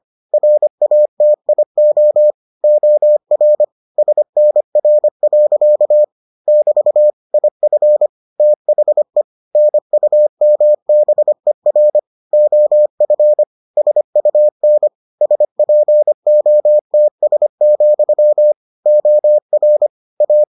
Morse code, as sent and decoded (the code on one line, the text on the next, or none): .-. .- - .. --- / --- .-. / ... -. .-. .-.-.- / -...- / .. ..-. / - .... / -. ..- -- -... .-. / --- ..-. / ... ..- -. / ... .--. --- - ... --..-- / --- .-. / .-
RATIO OR SNR. <BT> IF THE NUMBER OF SUN SPOTS, OR A